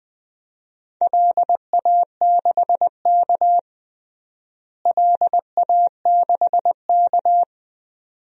Morse code, heard as a keyed tone, about 20 wpm, 700 Hz. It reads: LA6K LA6K
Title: DXCC-100 Call Signs - 4 Characters - Repeated 2x - 20wpm